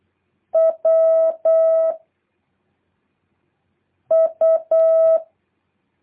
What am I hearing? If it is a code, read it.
Morse code, heard as a keyed tone, about 8 wpm, 650 Hz. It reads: W U
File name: jh8cbh / 2023cw